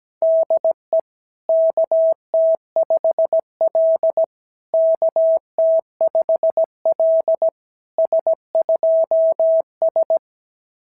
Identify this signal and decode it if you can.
Morse code, heard as a keyed tone, about 16 wpm, 650 Hz. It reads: DE KT5L KT5L S2S
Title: SOTA QSO Practice 17wpm